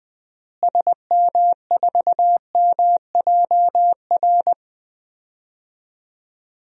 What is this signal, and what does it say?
Morse code, heard as a keyed tone, about 21 wpm, 700 Hz. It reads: SM4MJR
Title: DXCC-100 Call Signs - 6 Characters - Repeated 2x - 20wpm